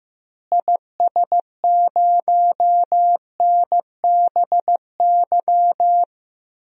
Morse code, heard as a keyed tone, about 15 wpm, 700 Hz. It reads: IS0NBY
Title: DXCC-100 Call Signs - 6 Characters - Repeated 3x - 15wpm